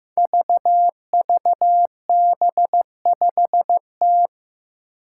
Morse code, heard as a keyed tone, about 15 wpm, 700 Hz. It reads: VVB5T
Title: Five Challenging Letter-Number Combinations 15wpm